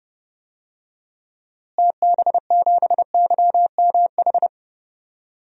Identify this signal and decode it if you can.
Morse code, heard as a keyed tone, about 30 wpm, 700 Hz. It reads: TB7YMH
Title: DXCC-100 Call Signs - 6 Characters - Repeated 3x - 30wpm